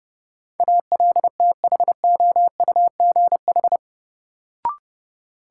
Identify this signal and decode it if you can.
Morse code, heard as a keyed tone, about 30 wpm, 700 Hz. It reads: ALTHOUGH
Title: Sentences from Top 600 Words - New Word Drill - 30wpm